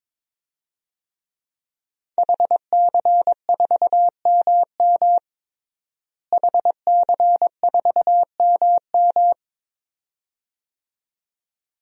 Morse code, heard as a keyed tone, about 21 wpm, 700 Hz. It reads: HC4MM HC4MM
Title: DXCC-100 Call Signs - 5 Characters - Repeated 2x - 22wpm